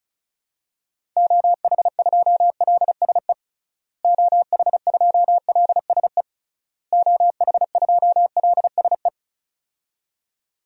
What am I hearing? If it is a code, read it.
Morse code, heard as a keyed tone, about 36 wpm, 700 Hz. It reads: OH2LSE OH2LSE OH2LSE
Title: DXCC-100 Call Signs - 6 Characters - Repeated 3x - 35wpm